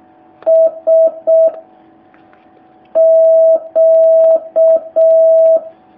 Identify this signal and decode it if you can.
Morse code, heard as a keyed tone, about 6 wpm, 650 Hz. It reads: S Q
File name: jh8cbh / 2019cw